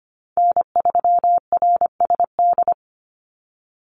Morse code, heard as a keyed tone, about 25 wpm, 700 Hz. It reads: N3RSD
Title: Call Signs Repeated 3x - 25wpm